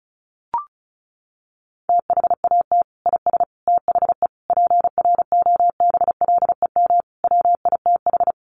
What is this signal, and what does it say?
Morse code, heard as a keyed tone, about 34 wpm, 700 Hz. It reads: THAT IS THE PROBLEM WITH
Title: Sentences from Top 1100 Words 35wpm